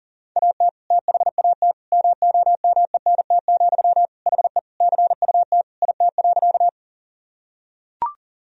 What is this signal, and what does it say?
Morse code, heard as a keyed tone, about 37 wpm, 700 Hz. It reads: AT THAT MOMENT, HE CUT IT.
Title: Sentences from Top 1000 Words - Review Part 1 - 40wpm